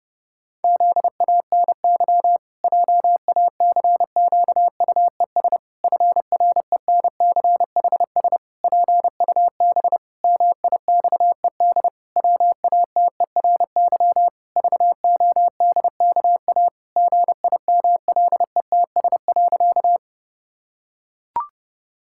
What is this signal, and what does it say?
Morse code, heard as a keyed tone, about 31 wpm, 700 Hz. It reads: ZANY JACQUES FRENCHS PUB MIXED WATERY VODKA GIMLETS.
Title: Pangram Challenge v2 - 30wpm